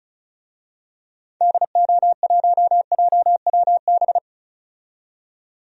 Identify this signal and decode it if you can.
Morse code, heard as a keyed tone, about 34 wpm, 700 Hz. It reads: DO1JWB
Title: DXCC-100 Call Signs - 6 Characters - 35wpm